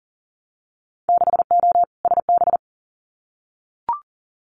Morse code, heard as a keyed tone, about 39 wpm, 700 Hz. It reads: 6O SB
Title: Winter Field Day Exchanges Encore 40wpm